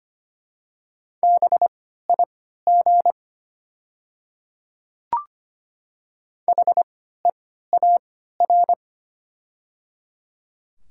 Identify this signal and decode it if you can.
Morse code, heard as a keyed tone, about 25 wpm, spaced as for 14 wpm, 700 Hz. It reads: BIG HEAR
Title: 3x Character Spacing - Top 300 Words - 25wpm